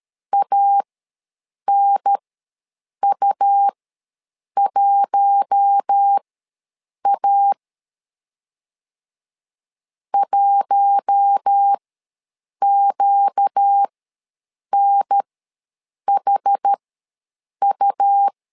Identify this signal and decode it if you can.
Morse code, heard as a keyed tone, about 13 wpm, spaced as for 7 wpm, 800 Hz. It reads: ANU1A 1QNHU
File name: LEZIONI / 4 Q1 ws 12wpm 2129